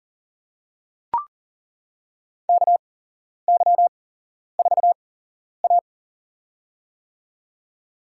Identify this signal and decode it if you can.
Morse code, heard as a keyed tone, about 40 wpm, spaced as for 11 wpm, 700 Hz. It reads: KY4A
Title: USA Call Signs - 4 Characters - 8x Character Spacing - 40wpm